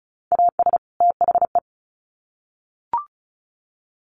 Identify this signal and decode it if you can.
Morse code, heard as a keyed tone, about 34 wpm, 700 Hz. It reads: AS THE
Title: Sets of 2 Words 35wpm